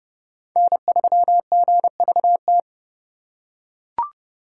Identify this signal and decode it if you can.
Morse code, heard as a keyed tone, about 30 wpm, 700 Hz. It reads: N3GVT
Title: USA Call Signs - 5 Characters - 2x Character Spacing - 30wpm